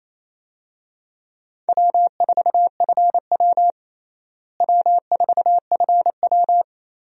Morse code, heard as a keyed tone, about 28 wpm, 700 Hz. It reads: W4FW W4FW
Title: USA Call Signs - 4 Characters - Repeated 2x - 28wpm